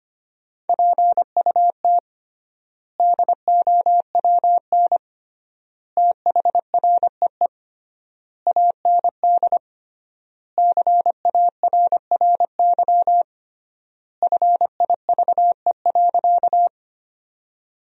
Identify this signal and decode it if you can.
Morse code, heard as a keyed tone, about 25 wpm, 700 Hz. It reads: PUT DOWN THREE AND CARRY FIVE.
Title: 3x Word-Spacing - Sentences from Top 400 Words - 25wpm